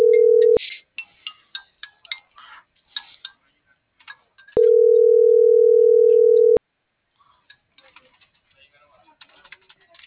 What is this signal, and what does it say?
Ringback tone (North America), 440 + 480 Hz.